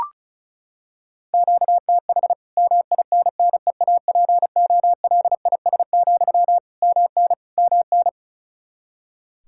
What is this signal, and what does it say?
Morse code, heard as a keyed tone, about 34 wpm, 700 Hz. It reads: QTH MINNEAPOLIS, MN MN
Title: Practice QSO - Exchange 1 - Beginning - 35wpm